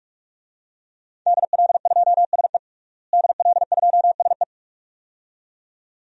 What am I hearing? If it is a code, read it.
Morse code, heard as a keyed tone, about 46 wpm, 700 Hz. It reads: DL2SE DL2SE
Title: DXCC-100 Call Signs - 5 Characters - Repeated 2x - 45wpm